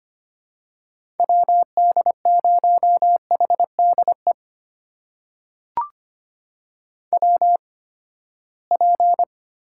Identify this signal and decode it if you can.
Morse code, heard as a keyed tone, about 25 wpm, spaced as for 17 wpm, 700 Hz. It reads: WD0HDE WP
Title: USA Call Signs - 6 Characters - 8x Character Spacing - 25wpm